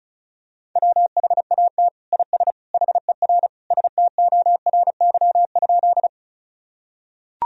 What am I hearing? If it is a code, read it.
Morse code, heard as a keyed tone, about 34 wpm, 700 Hz. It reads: WHAT IS HER STORY?
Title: Sentences from Top 1000 Words - Review Part 1 - 35wpm